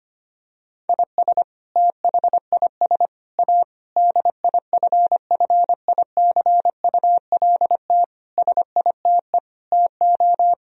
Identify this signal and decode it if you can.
Morse code, heard as a keyed tone, about 25 wpm, 700 Hz. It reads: IS THIS A DIFFICULT SITE TO
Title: Sentences from Top 1000 Words - Review Part 4 - 25wpm